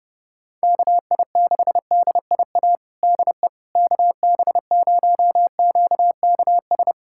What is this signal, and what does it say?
Morse code, heard as a keyed tone, about 30 wpm, 700 Hz. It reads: KI6DIA DE KB0QKS
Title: USA Callsign DE Callsign 30wpm